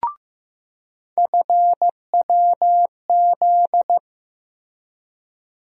Morse code, heard as a keyed tone, about 15 wpm, 700 Hz. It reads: FWZ
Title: USA Call Sign Postfixes - 3 Letters 15wpm